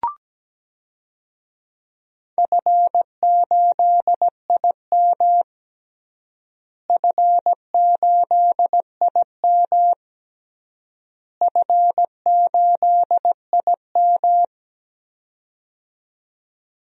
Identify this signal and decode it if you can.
Morse code, heard as a keyed tone, about 17 wpm, 700 Hz. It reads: F8IM F8IM F8IM
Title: DXCC-100 Call Signs - 4 Characters - Repeated 3x - 17wpm